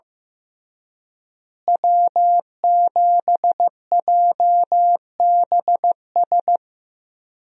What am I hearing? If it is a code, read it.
Morse code, heard as a keyed tone, about 15 wpm, 700 Hz. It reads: W7JBS
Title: USA Call Signs - 5 Characters - Repeated 2x - 15wpm